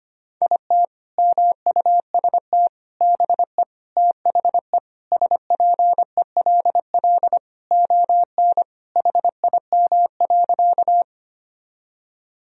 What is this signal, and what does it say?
Morse code, heard as a keyed tone, about 25 wpm, 700 Hz. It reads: IT MUST BE THE SPELL ON HIM.
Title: Sentences from Top 1000 Words - Review Part 1 - 25wpm